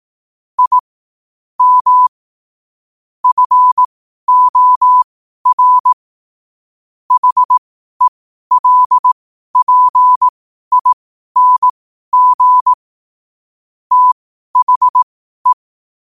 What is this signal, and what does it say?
Morse code, heard as a keyed tone, about 18 wpm, spaced as for 12 wpm, 1000 Hz. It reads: I M FOR HELPING THE
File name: qotdfiles / 12wpm